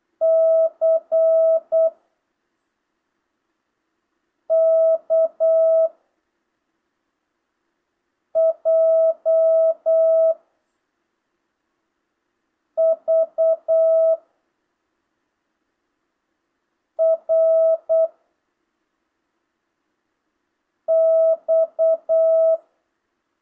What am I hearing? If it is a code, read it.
Morse code, heard as a keyed tone, about 8 wpm, 660 Hz. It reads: C K J V R X